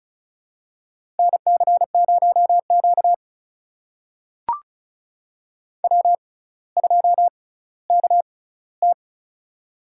Morse code, heard as a keyed tone, about 36 wpm, spaced as for 18 wpm, 700 Hz. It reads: NC0Q W2KT
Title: USA Call Signs - 4 Characters - 6x Character Spacing - 35wpm